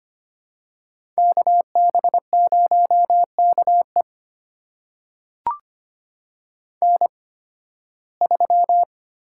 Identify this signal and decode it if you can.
Morse code, heard as a keyed tone, about 25 wpm, spaced as for 16 wpm, 700 Hz. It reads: KB0KE N3
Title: USA Call Signs - 5 Characters - 8x Character Spacing - 25wpm